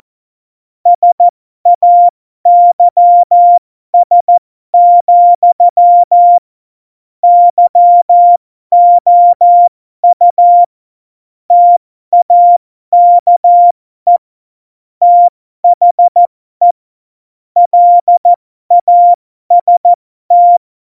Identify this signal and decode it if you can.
Morse code, heard as a keyed tone, about 14 wpm, spaced as for 12 wpm, 700 Hz. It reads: SAYS, YOU TAKE THE LAST